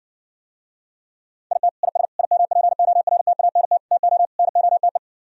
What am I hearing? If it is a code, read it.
Morse code, heard as a keyed tone, about 67 wpm, 700 Hz. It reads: IT IS IMPORTANT TO NOTE